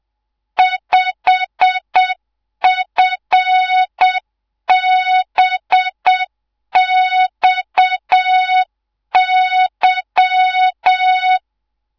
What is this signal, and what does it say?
Morse code, heard as a keyed tone, about 7 wpm, 3700 Hz, with noise. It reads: 5FBXY
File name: Morze-HA2MN / hangp / 2 03 GYAKORLAT